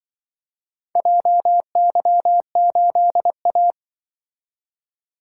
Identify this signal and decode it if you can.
Morse code, heard as a keyed tone, about 25 wpm, 700 Hz. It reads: JY8A